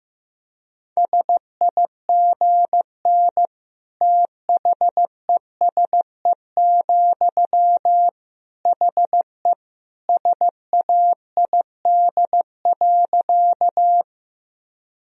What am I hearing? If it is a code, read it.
Morse code, heard as a keyed tone, about 15 wpm, 700 Hz. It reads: SIGN THESE, HE SAID.
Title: Sentences from Top 1000 Words - Review Part 1 - 15wpm